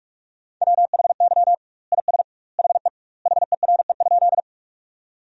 Morse code, heard as a keyed tone, about 46 wpm, 700 Hz. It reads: WHY IS HE HERE?
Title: 2x Word-Spacing - Sentences from Top 200 Words - 45wpm